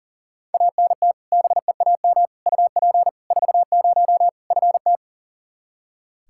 Morse code, heard as a keyed tone, about 40 wpm, 700 Hz. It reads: ANT BEAM UP 40 FT
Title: Practice QSO - Exchange 2 - Info - 40wpm